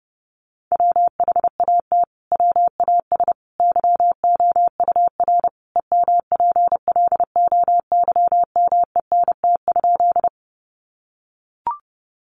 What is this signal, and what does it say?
Morse code, heard as a keyed tone, about 28 wpm, 700 Hz. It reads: WHAT WAS YOUR EMPLOYMENT?
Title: Sentences from Top 1500 Words 30wpm